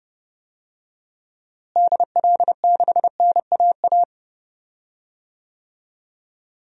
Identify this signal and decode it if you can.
Morse code, heard as a keyed tone, about 30 wpm, 700 Hz. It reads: DL6NAA
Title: DXCC-100 Call Signs - 6 Characters - Repeated 3x - 30wpm